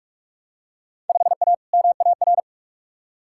Morse code, heard as a keyed tone, about 41 wpm, 700 Hz. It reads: 5A MAR
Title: Field Day Exchanges Encore - 45wpm